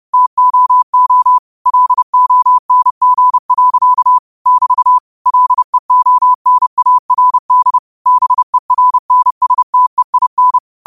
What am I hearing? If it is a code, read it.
Morse code, heard as a keyed tone, about 30 wpm, 1000 Hz. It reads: TOO LONG. <BT> LEONARD BERNSTEIN